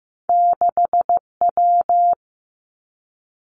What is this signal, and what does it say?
Morse code, heard as a keyed tone, about 15 wpm, 700 Hz. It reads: 6W
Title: Two Challenging Letter-Number Combinations 15wpm